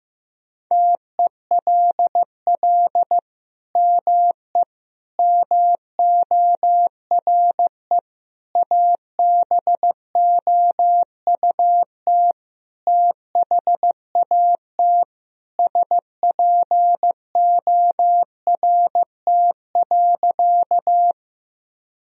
Morse code, heard as a keyed tone, about 16 wpm, 700 Hz. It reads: TELL ME MORE ABOUT THAT SPORT.
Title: Sentences from Top 1000 Words - Review Part 4 - 15wpm